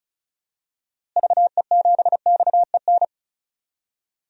Morse code, heard as a keyed tone, about 36 wpm, 700 Hz. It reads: VE7XEN
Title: DXCC-100 Call Signs - 6 Characters - Repeated 2x - 35wpm